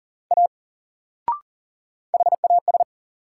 Morse code, heard as a keyed tone, about 40 wpm, 700 Hz. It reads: A HAS
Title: Sentences from Top 100 Words - New Word Drill - 40wpm